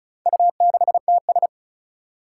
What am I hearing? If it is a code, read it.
Morse code, heard as a keyed tone, about 35 wpm, 700 Hz. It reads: U6TS
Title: Four Challenging Letter-Number Combinations 35wpm